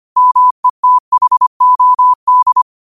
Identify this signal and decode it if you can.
Morse code, heard as a keyed tone, about 25 wpm, 1000 Hz. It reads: METHOD
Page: New quotes every day in morse code at 25 Words per minute.